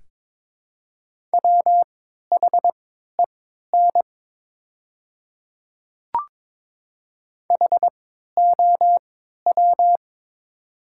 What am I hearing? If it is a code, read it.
Morse code, heard as a keyed tone, about 22 wpm, spaced as for 13 wpm, 700 Hz. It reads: WHEN HOW
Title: 3x Character Spacing - Top 100 Words - 22wpm